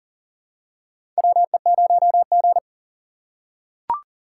Morse code, heard as a keyed tone, about 40 wpm, 700 Hz. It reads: WE0G
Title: USA Call Signs - 4 Characters - 3x Character Spacing - 40wpm